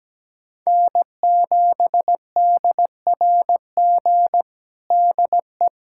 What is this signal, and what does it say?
Morse code, heard as a keyed tone, about 17 wpm, 700 Hz. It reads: N7DRG DE
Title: USA Callsign DE Callsign - 5 Characters - 17wpm